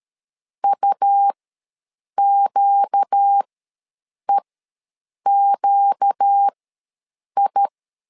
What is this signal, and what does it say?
Morse code, heard as a keyed tone, about 13 wpm, spaced as for 7 wpm, 800 Hz. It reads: UQEQI
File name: LEZIONI / 4 Q ws 12wpm 1912